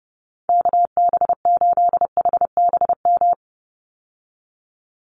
Morse code, heard as a keyed tone, about 30 wpm, 700 Hz. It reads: KB8HBM
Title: USA Call Signs 30wpm